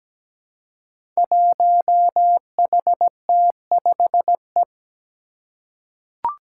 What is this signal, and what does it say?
Morse code, heard as a keyed tone, about 17 wpm, 700 Hz. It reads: 1HT5E
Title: Five Challenging Letter-Number Combinations 17wpm